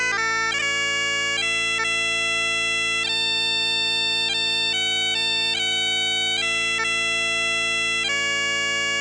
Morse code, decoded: none